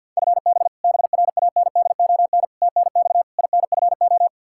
Morse code, heard as a keyed tone, about 47 wpm, 700 Hz. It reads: FB BRANDON TNX INFO
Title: Practice QSO - Exchange 3 - More Info - 50wpm